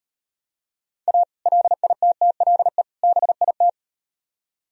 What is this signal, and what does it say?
Morse code, heard as a keyed tone, about 38 wpm, 700 Hz. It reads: A LITTLE BIT